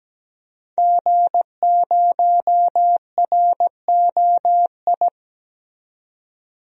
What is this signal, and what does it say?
Morse code, heard as a keyed tone, about 17 wpm, 700 Hz. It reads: G0ROI
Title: DXCC-100 Call Signs - 5 Characters - 17wpm